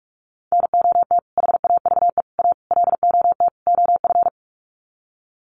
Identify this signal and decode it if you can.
Morse code, heard as a keyed tone, about 44 wpm, 700 Hz. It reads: NOT HAVE A LOT OF
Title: Sets of 5 Words 45wpm